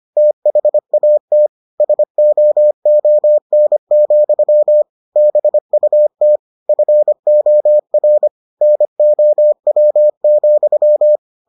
Morse code, heard as a keyed tone, about 25 wpm, 600 Hz. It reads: THAT SOON, BUT FOR NOW,